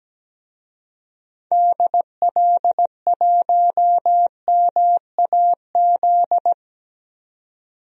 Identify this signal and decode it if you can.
Morse code, heard as a keyed tone, about 17 wpm, 700 Hz. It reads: DL1MAZ
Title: DXCC-100 Call Signs - 6 Characters - 17wpm